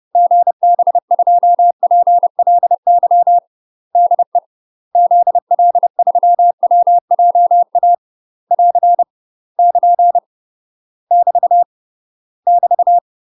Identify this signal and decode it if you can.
Morse code, heard as a keyed tone, about 30 wpm, 700 Hz. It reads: GB2PLY DE ZL3WJA <AR> <KN> <BT> <BT>
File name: UK MASTER QSOs TRAINING / 30WPM / UK MASTER QSOs 0019